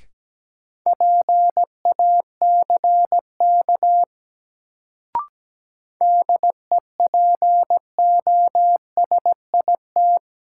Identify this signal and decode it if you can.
Morse code, heard as a keyed tone, about 17 wpm, 700 Hz. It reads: PACK DEPOSIT
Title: Sentences from Top 1300 Words - New Word Drill Alt - 17wpm